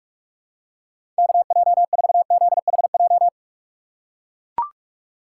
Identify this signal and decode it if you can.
Morse code, heard as a keyed tone, about 45 wpm, 690 Hz. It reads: KJ4ZHJ
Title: USA Call Signs - 6 Characters - 3x Character Spacing - 45wpm